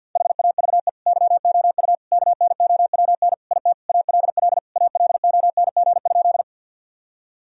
Morse code, heard as a keyed tone, about 52 wpm, 700 Hz. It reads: HAVE YOU KNOWN IT ALL ALONG?
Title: Sentences from Top 1000 Words - Review Part 3 - 50wpm